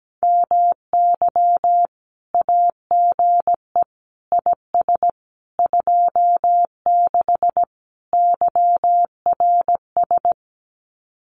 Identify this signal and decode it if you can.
Morse code, heard as a keyed tone, about 17 wpm, 700 Hz. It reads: MY AGE IS 26 YRS
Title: Common QSO Phrases Encore 17wpm